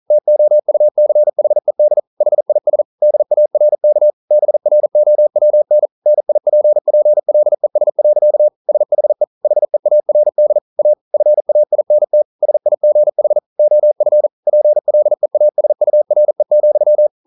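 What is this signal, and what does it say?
Morse code, heard as a keyed tone, about 41 wpm, 600 Hz. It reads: TOUCHED HIS DARK BROWN NIPPLES. SHE HEARD A FAINT SIGH OF PLEASURE,
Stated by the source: Morse code practice at 41 wpm, from CW med Gnister.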